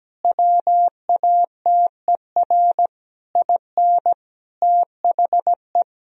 Morse code, heard as a keyed tone, about 17 wpm, 700 Hz. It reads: WATER IN THE
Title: Sentences from Top 1000 Words - Review Part 4 - 17wpm